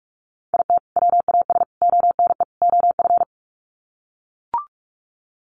Morse code, heard as a keyed tone, about 44 wpm, 700 Hz. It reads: IT WAS ONE OF
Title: Sets of 4 Words 45wpm